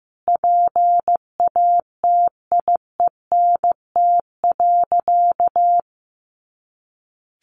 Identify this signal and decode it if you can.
Morse code, heard as a keyed tone, about 15 wpm, 700 Hz. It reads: PATIENT.
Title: Easy Phrases 15wpm